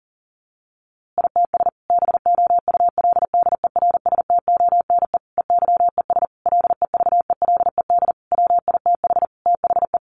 Morse code, heard as a keyed tone, about 40 wpm, 700 Hz. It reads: ITS BOULDERSTONE EYES LEVELED WITH THE
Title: A Sound Of Thunder 40wpm